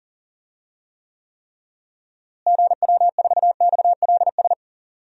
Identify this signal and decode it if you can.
Morse code, heard as a keyed tone, about 40 wpm, 700 Hz. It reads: GW4XLS
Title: DXCC-100 Call Signs - 6 Characters - Repeated 3x - 40wpm